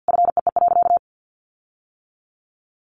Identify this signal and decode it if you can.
Morse code, heard as a keyed tone, about 50 wpm, 700 Hz. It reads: FEE.
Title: Sentences from Top 1500 Words 50wpm